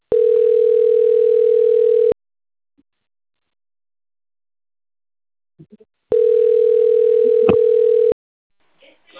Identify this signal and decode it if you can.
Ringback tone (North America), 440 + 480 Hz, with speech.
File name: out- - - - - .75442